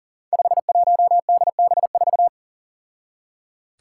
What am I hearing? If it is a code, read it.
Morse code, heard as a keyed tone, about 38 wpm, 700 Hz. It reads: 51DB4
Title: Five Challenging Letter-Number Combinations 40wpm